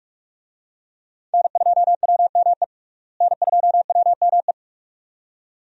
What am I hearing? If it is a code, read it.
Morse code, heard as a keyed tone, about 44 wpm, 700 Hz. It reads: N2WME N2WME
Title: USA Call Signs - 5 Characters - Repeated 2x - 45wpm